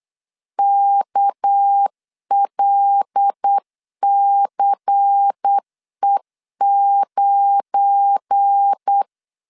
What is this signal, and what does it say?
Morse code, heard as a keyed tone, about 8 wpm, 800 Hz. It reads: KLCE9
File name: LEZIONI / 10 SIGNS / 10A SLASH 8wpm 3452